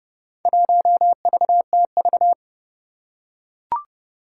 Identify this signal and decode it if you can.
Morse code, heard as a keyed tone, about 30 wpm, 700 Hz. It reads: 1VTV
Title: Four Challenging Letter-Number Combinations 30wpm